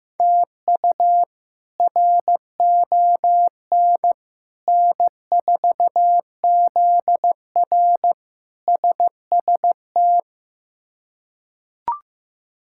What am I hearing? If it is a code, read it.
Morse code, heard as a keyed tone, about 15 wpm, 700 Hz. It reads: TU RON N4ZR SST
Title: K1USN SST QSO Practice - 15wpm